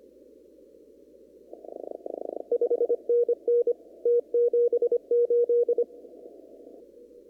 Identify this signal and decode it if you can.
Morse code, heard as a keyed tone, about 25 wpm, 2400 Hz. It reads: 5NN T78
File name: CW / kt5j wpx cw-op-k5pi